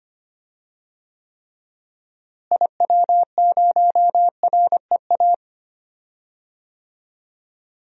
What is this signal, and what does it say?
Morse code, heard as a keyed tone, about 25 wpm, 700 Hz. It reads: IW0REA
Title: DXCC-100 Call Signs - 6 Characters - Repeated 2x - 25wpm